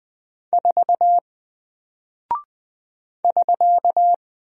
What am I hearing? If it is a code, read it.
Morse code, heard as a keyed tone, about 20 wpm, 700 Hz. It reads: 4 <SK>
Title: Single Character (Letter:Number:Punctuation:Prosign) 20wpm